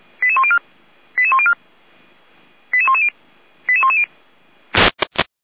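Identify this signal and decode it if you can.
Five-tone selective call (ZVEI), heard as quick runs of five tones, tones of about 70 ms. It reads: 88104 88104 88110 88110
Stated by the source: ZVEI five-tone selective call received on an SDR.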